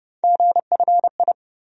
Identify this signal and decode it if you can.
Morse code, heard as a keyed tone, about 30 wpm, 700 Hz. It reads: GFI
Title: USA Call Sign Postfixes - 3 Letters 30wpm